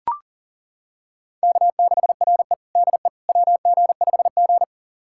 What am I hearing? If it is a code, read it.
Morse code, heard as a keyed tone, about 40 wpm, 700 Hz. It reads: K6RE DE WG5G
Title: USA Callsign DE Callsign - 4 Characters - 40wpm